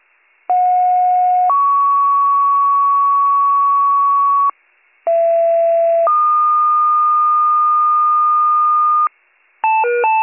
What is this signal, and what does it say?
Two-tone page, heard as a one-second tone, then a three-second tone, about 730 then 1120 Hz, about 670 then 1220 Hz. It is heard from 0.5 to 9.1 s.